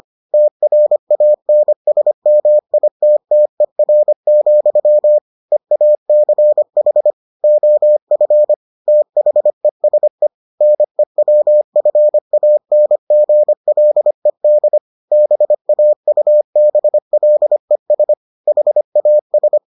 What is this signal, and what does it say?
Morse code, heard as a keyed tone, about 25 wpm, 600 Hz. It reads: TRANSMITTER, EACH OF THESE NEWFANGLED BAUBLES HAS